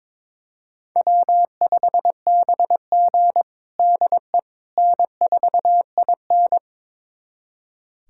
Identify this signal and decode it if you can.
Morse code, heard as a keyed tone, about 22 wpm, 700 Hz. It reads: W5BG DE N4IN
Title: USA Callsign DE Callsign - 4 Characters - 22wpm